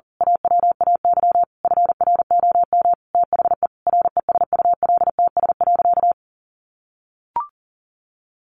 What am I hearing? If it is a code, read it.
Morse code, heard as a keyed tone, about 40 wpm, 700 Hz. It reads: AWAY FROM THE RESULTS.
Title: A Sound Of Thunder 40wpm